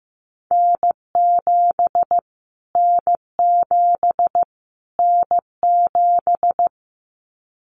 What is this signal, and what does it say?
Morse code, heard as a keyed tone, about 15 wpm, 700 Hz. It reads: N7 N7 N7
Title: Callsign Prefixes for Top 25 DXCC - Repeated 3x - 15wpm